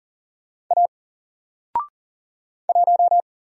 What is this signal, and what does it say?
Morse code, heard as a keyed tone, about 40 wpm, 700 Hz. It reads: A 1
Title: Single Letter-Number 40wpm